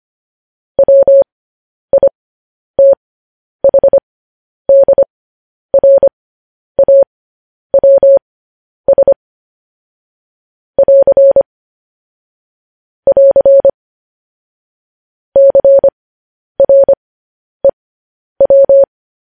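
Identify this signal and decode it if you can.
Morse code, heard as a keyed tone, about 25 wpm, spaced as for 10 wpm, 550 Hz. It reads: WITHDRAWS <AR> <AR> CREW
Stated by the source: Morse code news headlines broadcast.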